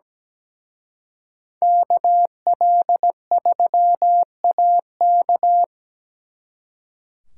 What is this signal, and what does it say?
Morse code, heard as a keyed tone, about 20 wpm, 700 Hz. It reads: KL3AK
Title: Speed Racing - USA 2x2 Call Signs 17wpm